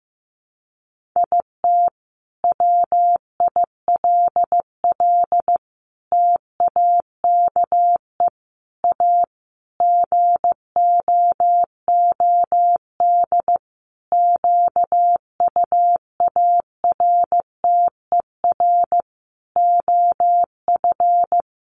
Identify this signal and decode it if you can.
Morse code, heard as a keyed tone, about 15 wpm, 700 Hz. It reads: IT WILL TAKE A GOOD QUARTER OF
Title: Sentences from Top 1300 Words 15wpm